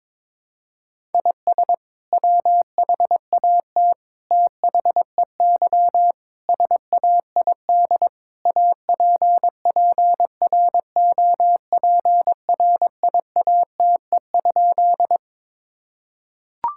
Morse code, heard as a keyed tone, about 22 wpm, 700 Hz. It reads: IS WHAT THEY SAID APPROPRIATE?
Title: Sentences from Top 1000 Words - Review Part 5 - 22wpm